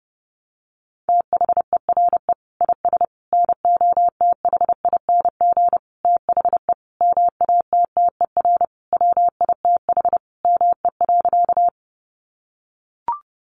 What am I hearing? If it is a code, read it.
Morse code, heard as a keyed tone, about 30 wpm, 700 Hz. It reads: THERE IS NOTHING THE MATTER WITH ME.
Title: Sentences from Top 700 Words 30wpm